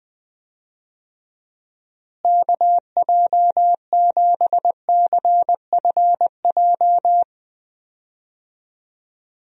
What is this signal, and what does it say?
Morse code, heard as a keyed tone, about 20 wpm, 700 Hz. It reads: KJ7CFJ
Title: USA Call Signs - 6 Characters - Repeated 2x - 20wpm